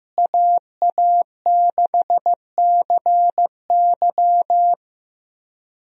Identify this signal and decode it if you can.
Morse code, heard as a keyed tone, about 15 wpm, 700 Hz. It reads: AA6CY
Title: USA Call Signs - 5 Characters - Repeated 3x - 15wpm